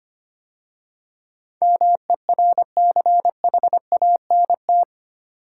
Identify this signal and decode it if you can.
Morse code, heard as a keyed tone, about 25 wpm, 700 Hz. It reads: MERCHANT
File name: 8 Letter Words - First 3 Letters Given - 25wpm